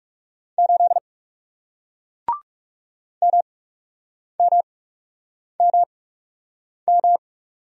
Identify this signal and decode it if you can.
Morse code, heard as a keyed tone, about 40 wpm, spaced as for 8 wpm, 700 Hz. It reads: 8 MMMM